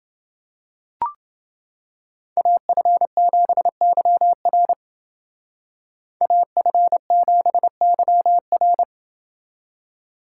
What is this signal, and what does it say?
Morse code, heard as a keyed tone, about 29 wpm, 700 Hz. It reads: AF7YR AF7YR